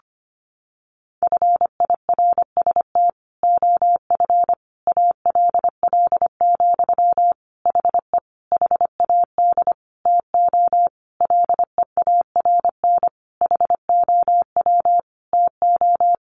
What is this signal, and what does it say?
Morse code, heard as a keyed tone, about 25 wpm, 700 Hz. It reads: FIRST OF ALL, HE HAD TO LEARN HOW TO
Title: Sentences from Top 1200 Words 25wpm